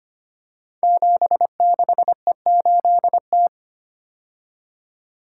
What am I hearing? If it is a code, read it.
Morse code, heard as a keyed tone, about 25 wpm, 700 Hz. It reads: 76E8T
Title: Code Groups - 5 Letter:Numbers - 25wpm